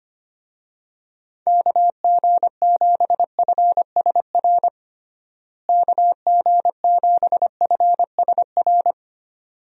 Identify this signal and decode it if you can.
Morse code, heard as a keyed tone, about 25 wpm, 700 Hz. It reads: KG7FSR KG7FSR
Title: USA Call Signs - 6 Characters - Repeated 2x - 25wpm